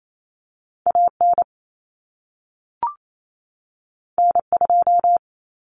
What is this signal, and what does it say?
Morse code, heard as a keyed tone, about 28 wpm, 700 Hz. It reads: AN N2
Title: Common Cut Numbers - 2-Digit 28wpm